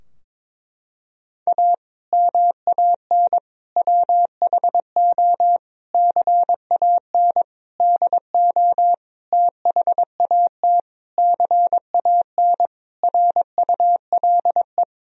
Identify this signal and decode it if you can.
Morse code, heard as a keyed tone, about 22 wpm, 700 Hz. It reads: A MAN WHO CAN DO THAT CAN RULE
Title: Sentences from Top 1000 Words - Review Part 2 - 22wpm